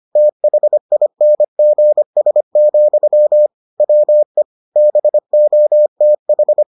Morse code, heard as a keyed tone, about 25 wpm, 600 Hz. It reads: THINGS, WE BOTH